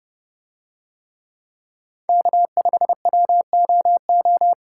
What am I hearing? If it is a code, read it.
Morse code, heard as a keyed tone, about 29 wpm, 700 Hz. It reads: K5WOO